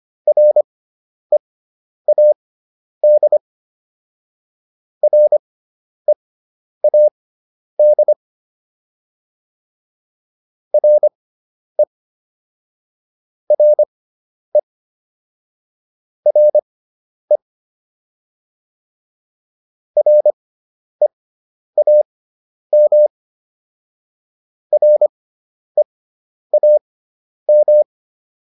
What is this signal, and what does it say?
Morse code, heard as a keyed tone, about 25 wpm, spaced as for 10 wpm, 600 Hz. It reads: READ READ RE RE RE REAM REAM